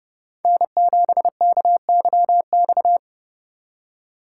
Morse code, heard as a keyed tone, about 32 wpm, 700 Hz. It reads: N7KYX